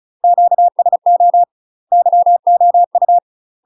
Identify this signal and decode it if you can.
Morse code, heard as a keyed tone, about 35 wpm, 700 Hz. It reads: QSO YOU
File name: UK MASTER QSOs TRAINING / 35WPM / UK MASTER QSOs 0026